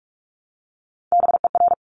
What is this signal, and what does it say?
Morse code, heard as a keyed tone, about 46 wpm, 700 Hz. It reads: BER
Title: Top Three Letter Combinations in Words 45wpm